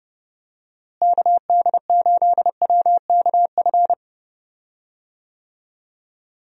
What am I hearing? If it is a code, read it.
Morse code, heard as a keyed tone, about 30 wpm, 700 Hz. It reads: KD8WKF
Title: USA Call Signs - 6 Characters - Repeated 3x - 30wpm